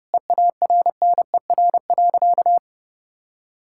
Morse code, heard as a keyed tone, about 30 wpm, 700 Hz. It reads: EARNER.
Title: Pangram Challenge v2 - 30wpm